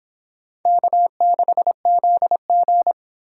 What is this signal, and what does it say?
Morse code, heard as a keyed tone, about 24 wpm, 700 Hz. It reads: K6ZG